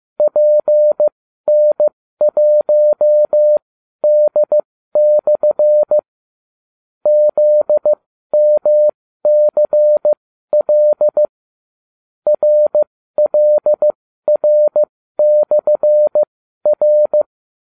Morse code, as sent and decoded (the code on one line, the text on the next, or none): .--. -. .---- -.. -..-. / --.. -- -.-. .-.. / .-. .-.. .-. -..-. .-.
PN1D/ ZMCL RLR/R